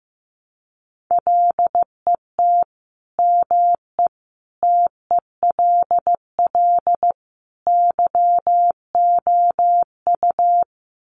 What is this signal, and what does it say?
Morse code, heard as a keyed tone, about 15 wpm, 700 Hz. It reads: LET ME TELL YOU
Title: Sets of 4 Words 15wpm